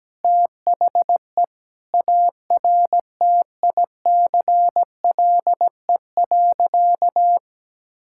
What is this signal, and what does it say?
Morse code, heard as a keyed tone, about 17 wpm, 700 Hz. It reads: THE ARTICLE.
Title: Sentences from Top 1000 Words - Review Part 5 - 17wpm